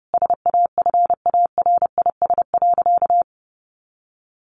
Morse code, heard as a keyed tone, about 28 wpm, 700 Hz. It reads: SAFARIS.
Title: A Sound Of Thunder 30wpm